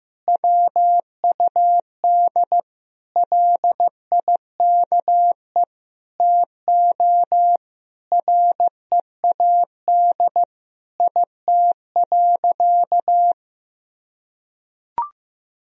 Morse code, heard as a keyed tone, about 15 wpm, 700 Hz. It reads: WUD LIKE TO READ IT.
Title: Sentences from Top 500 Words and Abbreviations 15wpm